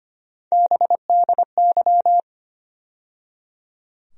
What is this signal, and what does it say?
Morse code, heard as a keyed tone, about 25 wpm, 700 Hz. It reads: BDY
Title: USA Call Sign Postfixes - 3 Letters 25wpm